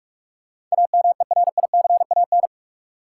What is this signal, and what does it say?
Morse code, heard as a keyed tone, about 47 wpm, 700 Hz. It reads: AMERICAN
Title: Sets of 2 Words - New Word Drill - 45wpm